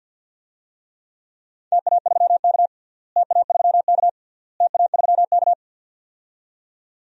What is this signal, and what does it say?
Morse code, heard as a keyed tone, about 47 wpm, 700 Hz. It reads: TA3K TA3K TA3K
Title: DXCC-100 Call Signs - 4 Characters - Repeated 3x - 50wpm